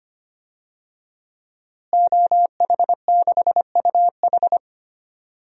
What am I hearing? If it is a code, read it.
Morse code, heard as a keyed tone, about 25 wpm, 700 Hz. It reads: OH6UH
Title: DXCC-100 Call Signs - 5 Characters - Repeated 2x - 25wpm